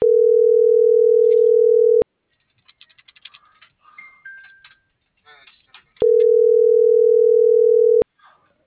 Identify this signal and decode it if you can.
Ringback tone (North America), 440 + 480 Hz.